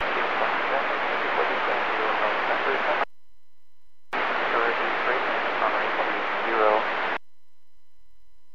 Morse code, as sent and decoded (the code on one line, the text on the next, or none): none